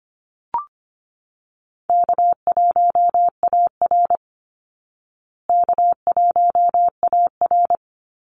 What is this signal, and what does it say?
Morse code, heard as a keyed tone, about 25 wpm, 700 Hz. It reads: K1AR K1AR
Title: Easy Call Signs Repeated 3x - 25wpm